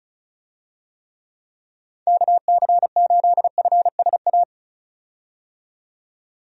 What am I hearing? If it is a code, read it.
Morse code, heard as a keyed tone, about 36 wpm, 700 Hz. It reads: KC8FSA
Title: USA Call Signs - 6 Characters - 35wpm